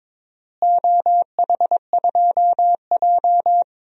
Morse code, heard as a keyed tone, about 22 wpm, 700 Hz. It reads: OH2J
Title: DXCC-100 Call Signs - 4 Characters - Repeated 3x - 22wpm